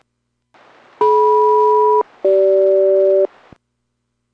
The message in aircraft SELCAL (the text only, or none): DM-CG